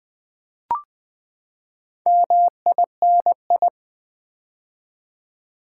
Morse code, heard as a keyed tone, about 20 wpm, 700 Hz. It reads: MINI
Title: Build A Word - 4 Letters At A Time - 8-Letter Words - 20wpm